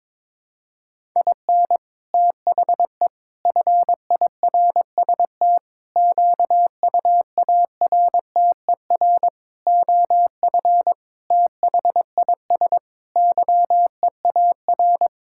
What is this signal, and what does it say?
Morse code, heard as a keyed tone, about 22 wpm, 700 Hz. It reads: IN THE FIRST QUARTER OF THIS YEAR
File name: practice / 22wpm / Sets of 7 Words - 22wpm